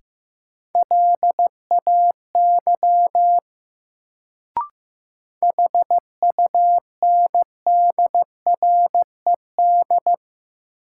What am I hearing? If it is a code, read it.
Morse code, heard as a keyed tone, about 15 wpm, 700 Hz. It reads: LAY HUNDRED
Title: Sentences from Top 400 Words - New Word Drill - 15wpm